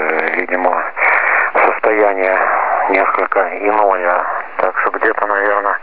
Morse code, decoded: none

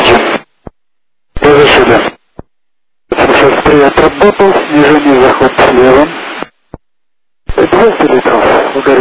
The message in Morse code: none

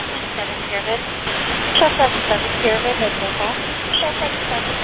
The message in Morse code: none

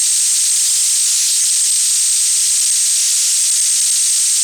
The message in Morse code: none

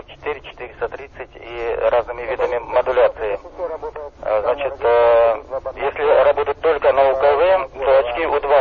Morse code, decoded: none

